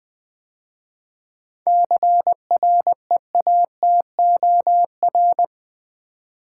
Morse code, heard as a keyed tone, about 20 wpm, 700 Hz. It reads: CREATOR
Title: 7 Letter Words - First 3 Letters Given - 20wpm